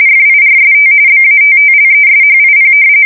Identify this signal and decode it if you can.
RTTY, about 45.45 baud. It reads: FIRST BATTLESHIP
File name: fsk170 / battleshipnj